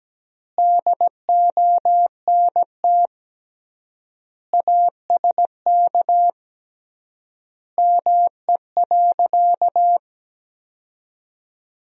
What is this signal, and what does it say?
Morse code, heard as a keyed tone, about 17 wpm, 700 Hz. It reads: DONT ASK ME.
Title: 3x Word-Spacing - Sentences from Top 500 Words - 17wpm